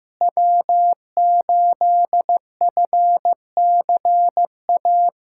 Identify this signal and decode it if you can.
Morse code, heard as a keyed tone, about 15 wpm, 700 Hz. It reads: W8FCA
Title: USA Callsign DE Callsign - 5 Characters - 15wpm